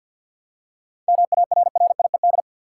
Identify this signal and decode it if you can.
Morse code, heard as a keyed tone, about 51 wpm, 700 Hz. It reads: MARRIED
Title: Seven Letter Words - 50wpm